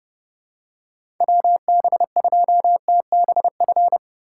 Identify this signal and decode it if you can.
Morse code, heard as a keyed tone, about 30 wpm, 700 Hz. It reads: WB2TBF
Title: USA Call Signs - 6 Characters - 4x Character Spacing - 30wpm